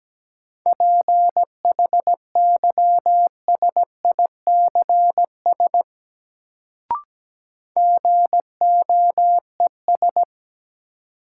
Morse code, heard as a keyed tone, about 17 wpm, 700 Hz. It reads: PHYSICS GOES
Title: Sentences from Top 1500 Words - New Word Drill Alt - 17wpm